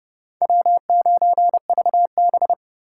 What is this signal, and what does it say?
Morse code, heard as a keyed tone, about 30 wpm, 700 Hz. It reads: W9VB
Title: USA Call Signs - 4x Character Spacing - 30wpm